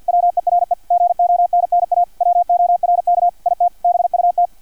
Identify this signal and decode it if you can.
Morse code, heard as a keyed tone, about 50 wpm, 700 Hz, with radio noise. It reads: WERE GONNA WORK IT BUT
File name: archived / CRSnet